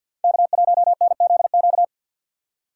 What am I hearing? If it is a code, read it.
Morse code, heard as a keyed tone, about 52 wpm, 700 Hz. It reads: K1NZQ
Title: USA Call Signs - 5 Characters - 6x Character Spacing - 50wpm